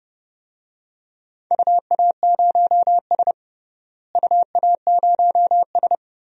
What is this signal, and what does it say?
Morse code, heard as a keyed tone, about 30 wpm, 700 Hz. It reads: UA0S UA0S
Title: DXCC-100 Call Signs - 4 Characters - Repeated 3x - 30wpm